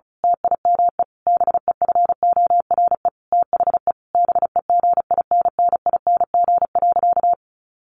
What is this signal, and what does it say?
Morse code, heard as a keyed tone, about 34 wpm, 700 Hz. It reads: TIME BEFORE THE BEGINNING.
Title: A Sound Of Thunder 35wpm